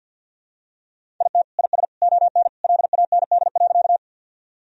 Morse code, heard as a keyed tone, about 49 wpm, 700 Hz. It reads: IT IS ON LAND.
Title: Sentences from Top 1000 Words - Review Part 1 - 50wpm